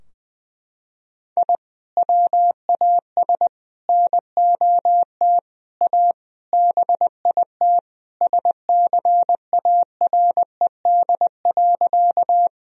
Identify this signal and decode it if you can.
Morse code, heard as a keyed tone, about 20 wpm, 700 Hz. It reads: I WAS NOT A BIT SCARED.
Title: Sentences from Top 1000 Words - Review Part 4 - 20wpm